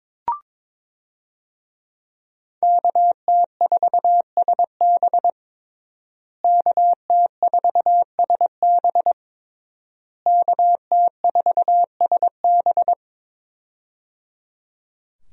Morse code, heard as a keyed tone, about 21 wpm, 700 Hz. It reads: KT4SB KT4SB KT4SB
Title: USA Call Signs - 5 Characters - Repeated 3x - 22wpm